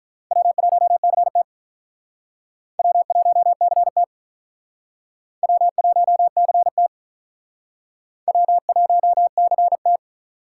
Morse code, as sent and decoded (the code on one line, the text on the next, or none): .-- .---- -.-. - / .-- .---- -.-. - / .-- .---- -.-. - / .-- .---- -.-. -
W1CT W1CT W1CT W1CT